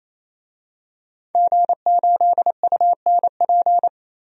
Morse code, heard as a keyed tone, about 28 wpm, 700 Hz. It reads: G8UNP
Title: DXCC-100 Call Signs - 5 Characters - 28wpm